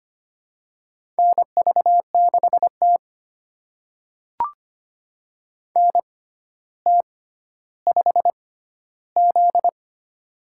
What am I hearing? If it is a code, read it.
Morse code, heard as a keyed tone, about 25 wpm, spaced as for 13 wpm, 700 Hz. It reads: NV6T NT5Z
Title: USA Call Signs - 4 Characters - 6x Character Spacing - 25wpm